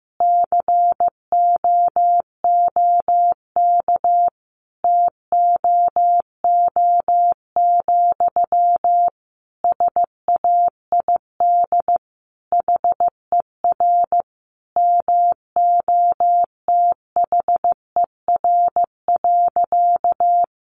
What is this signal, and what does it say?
Morse code, heard as a keyed tone, about 15 wpm, 700 Hz. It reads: COOK TOO, SAID HER MOTHER.
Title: Sentences from Top 700 Words 15wpm